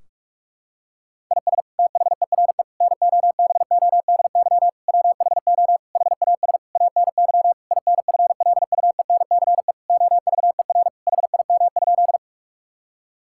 Morse code, heard as a keyed tone, about 41 wpm, 700 Hz. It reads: IS THERE NOBODY WHO HAS ANY INFLUENCE OVER HIM?
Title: Sentences from Top 1600 Words 45wpm